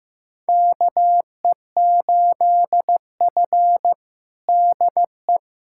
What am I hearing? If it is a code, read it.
Morse code, heard as a keyed tone, about 15 wpm, 700 Hz. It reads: KE8F DE
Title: USA Callsign DE Callsign - 4 Characters - 15wpm